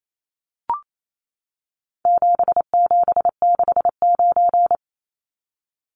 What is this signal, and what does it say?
Morse code, heard as a keyed tone, about 28 wpm, 700 Hz. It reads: 7769
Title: Numbers - 4-Digit 28wpm